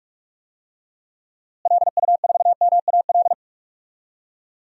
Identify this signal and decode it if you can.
Morse code, heard as a keyed tone, about 46 wpm, 700 Hz. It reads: LU4MAL
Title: DXCC-100 Call Signs - 6 Characters - Repeated 3x - 45wpm